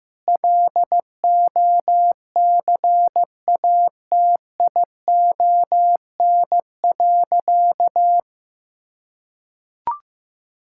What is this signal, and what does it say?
Morse code, heard as a keyed tone, about 15 wpm, 700 Hz. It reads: LOCATION.
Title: Sentences from Top 1000 Words - Review Part 6 - 15wpm